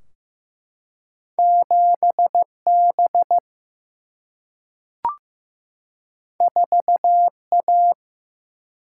Morse code, heard as a keyed tone, about 15 wpm, 700 Hz. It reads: 7 B 4 A
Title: Single-Letter Single-Number 15wpm